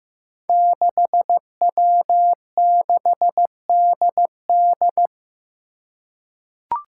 Morse code, heard as a keyed tone, about 15 wpm, 700 Hz. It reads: 6W6DD
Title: Five Challenging Letter-Number Combinations 15wpm